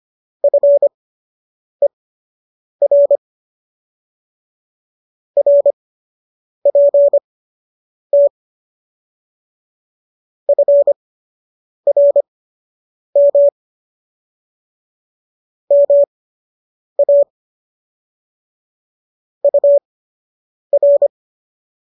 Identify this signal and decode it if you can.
Morse code, heard as a keyed tone, about 25 wpm, spaced as for 8 wpm, 570 Hz. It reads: FER RPT FRM MA UR